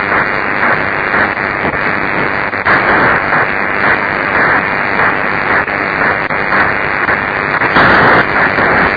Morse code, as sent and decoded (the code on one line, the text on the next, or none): none